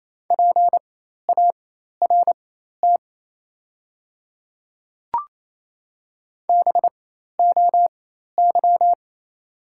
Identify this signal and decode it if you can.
Morse code, heard as a keyed tone, about 27 wpm, spaced as for 13 wpm, 690 Hz. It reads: PART BOY
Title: 4x Character Spacing - Top 500 Words - 28wpm